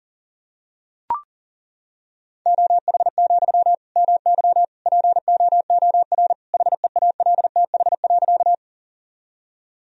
Morse code, heard as a keyed tone, about 40 wpm, 700 Hz. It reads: OH, MY POOR HEALTH.
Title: Sentences from Top 1000 Words - Review Part 3 - 40wpm